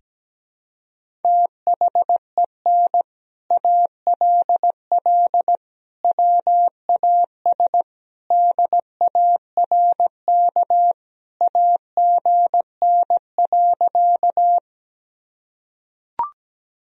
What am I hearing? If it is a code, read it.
Morse code, heard as a keyed tone, about 17 wpm, 700 Hz. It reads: THEN ALL WAS DARK AGN.
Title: Sentences from Top 500 Words and Abbreviations 17wpm